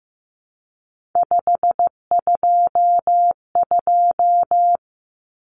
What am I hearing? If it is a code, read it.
Morse code, heard as a keyed tone, about 15 wpm, 700 Hz. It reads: 522
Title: Numbers - 3-Digit 15wpm